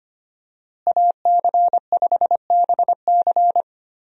Morse code, heard as a keyed tone, about 25 wpm, 700 Hz. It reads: AC5BC
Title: USA Call Signs - 5 Characters - Repeated 2x - 25wpm